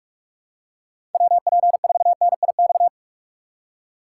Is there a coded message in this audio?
Yes: Morse code from 1.1 to 2.9 s, heard as a keyed tone, about 44 wpm, 700 Hz. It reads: WP4NIX